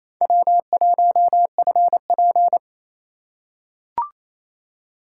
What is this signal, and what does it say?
Morse code, heard as a keyed tone, about 28 wpm, 700 Hz. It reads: W1FP
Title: USA Call Signs - 4 Characters - 2x Character Spacing - 28wpm